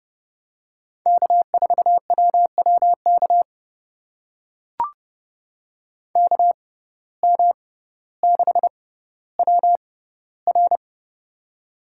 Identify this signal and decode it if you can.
Morse code, heard as a keyed tone, about 30 wpm, spaced as for 15 wpm, 700 Hz. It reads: K4WWK KM6WR
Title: USA Call Signs - 5 Characters - 6x Character Spacing - 30wpm